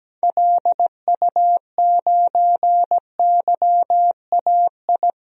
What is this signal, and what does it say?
Morse code, heard as a keyed tone, about 17 wpm, 700 Hz. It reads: LU9YAI
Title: DXCC-100 Call Signs - 6 Characters - 17wpm